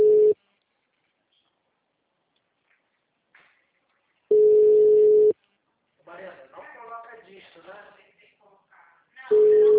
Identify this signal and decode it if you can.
Ringback tone (Europe, 425 Hz), with speech.